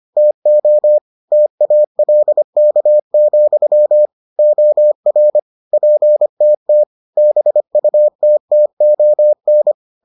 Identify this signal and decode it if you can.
Morse code, heard as a keyed tone, about 25 wpm, 600 Hz. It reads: TO TALK, OR PTT BUTTON